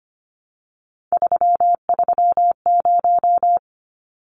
Morse code, heard as a keyed tone, about 25 wpm, 700 Hz. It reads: 330
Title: Numbers - 3-Digit 25wpm